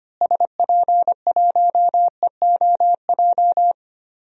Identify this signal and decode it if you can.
Morse code, heard as a keyed tone, about 25 wpm, 700 Hz. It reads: SP1EOJ
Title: DXCC-100 Call Signs - 6 Characters - Repeated 3x - 25wpm